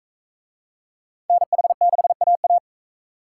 Morse code, heard as a keyed tone, about 37 wpm, 700 Hz. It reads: NH6AA